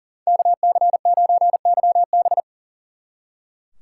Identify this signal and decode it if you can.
Morse code, heard as a keyed tone, about 49 wpm, 700 Hz. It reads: KC9YB
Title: Speed Racing - USA 2x2 Call Signs 40wpm